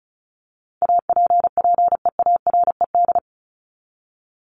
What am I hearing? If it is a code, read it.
Morse code, heard as a keyed tone, about 34 wpm, 700 Hz. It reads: APPEARED
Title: ED Suffix - Top 100 Words with Base Words 35wpm